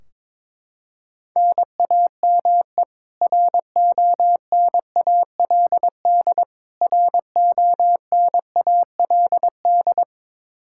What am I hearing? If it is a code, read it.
Morse code, heard as a keyed tone, about 22 wpm, 700 Hz. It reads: NAME RONALD RONALD
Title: Practice QSO - Exchange 1 - Beginning - 22wpm